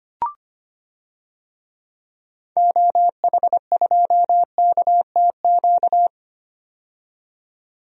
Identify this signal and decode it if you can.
Morse code, heard as a keyed tone, about 25 wpm, 700 Hz. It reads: OH2KTQ
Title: DXCC-100 Call Signs - 6 Characters - 25wpm